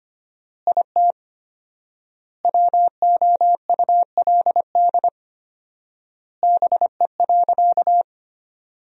Morse code, heard as a keyed tone, about 25 wpm, 700 Hz. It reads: IT WOULD BE.
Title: 4x Word-Spacing - Sentences from Top 200 Words - 25wpm